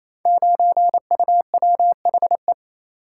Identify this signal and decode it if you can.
Morse code, heard as a keyed tone, about 27 wpm, 700 Hz. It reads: 9UWHE
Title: Code Groups - 5 Letter:Numbers - 28wpm